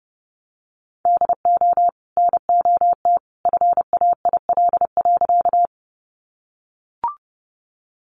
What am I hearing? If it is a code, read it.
Morse code, heard as a keyed tone, about 30 wpm, 700 Hz. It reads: DO NOT FAIL.
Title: Sentences from Top 1100 Words 30wpm